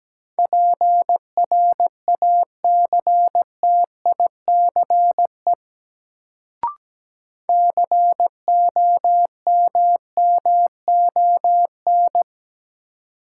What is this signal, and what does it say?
Morse code, heard as a keyed tone, about 17 wpm, 700 Hz. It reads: PRACTICE COMMON
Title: Sentences from Top 700 Words - New Word Drill - 17wpm